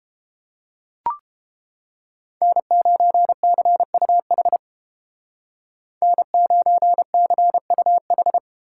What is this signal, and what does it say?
Morse code, heard as a keyed tone, about 30 wpm, 700 Hz. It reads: N9CUH N9CUH